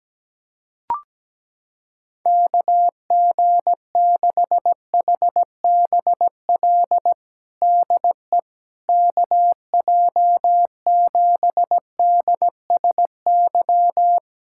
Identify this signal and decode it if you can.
Morse code, heard as a keyed tone, about 17 wpm, 700 Hz. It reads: KG6HBL DE KJ7DSY
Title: K1USN SST QSO Practice 17wpm